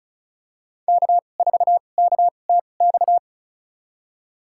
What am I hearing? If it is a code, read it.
Morse code, heard as a keyed tone, about 36 wpm, spaced as for 25 wpm, 700 Hz. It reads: K4KTX
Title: USA Call Signs - 2x Character Spacing - 35wpm